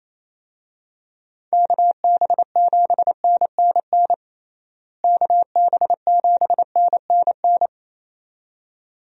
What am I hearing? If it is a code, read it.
Morse code, heard as a keyed tone, about 28 wpm, 700 Hz. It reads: KB7NNN KB7NNN
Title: USA Call Signs - 6 Characters - Repeated 2x - 28wpm